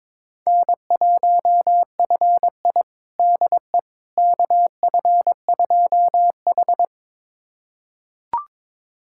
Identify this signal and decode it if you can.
Morse code, heard as a keyed tone, about 22 wpm, 700 Hz. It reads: N1FI DE KF2H
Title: USA Callsign DE Callsign - 4 Characters - 22wpm